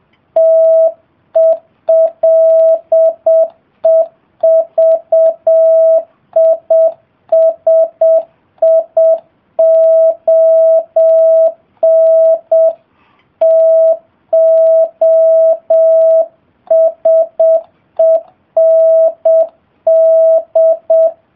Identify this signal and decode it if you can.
Morse code, heard as a keyed tone, about 7 wpm, 650 Hz. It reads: TELEVISION TO SEND